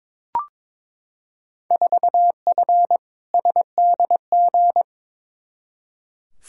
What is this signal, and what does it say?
Morse code, heard as a keyed tone, about 22 wpm, 700 Hz. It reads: 4F SDG
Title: Field Day Exchanges - 22wpm